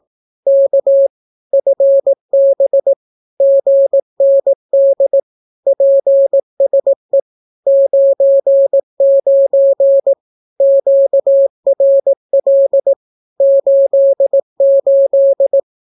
Morse code, heard as a keyed tone, about 18 wpm, 550 Hz. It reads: K FB GND PSE 99 QRL 88
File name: naukapodstaw / skroty / skroty